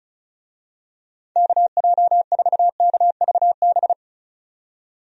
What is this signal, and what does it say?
Morse code, heard as a keyed tone, about 34 wpm, 700 Hz. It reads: KJ4KVB
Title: USA Call Signs - 6 Characters - 35wpm